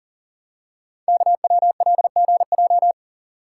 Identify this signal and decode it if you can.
Morse code, heard as a keyed tone, about 42 wpm, 700 Hz. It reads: KWLGJ
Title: Code Groups - 5 Letters - 40wpm